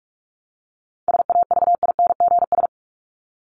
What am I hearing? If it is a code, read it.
Morse code, heard as a keyed tone, about 44 wpm, 700 Hz. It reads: SAVINGS
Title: Top 1001-1250 Words 45wpm